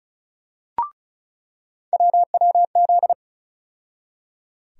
Morse code, heard as a keyed tone, about 36 wpm, 700 Hz. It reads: WWZ
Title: USA Call Sign Postfixes - 3 Letters 35wpm